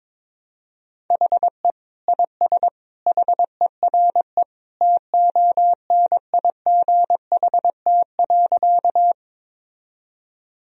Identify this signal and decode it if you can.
Morse code, heard as a keyed tone, about 22 wpm, 700 Hz. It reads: HE IS HERE TONIGHT.
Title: Sentences from Top 1000 Words - Review Part 3 - 22wpm